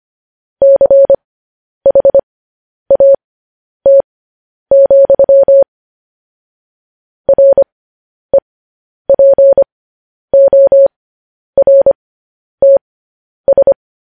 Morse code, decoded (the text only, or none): CHAT, REPORTS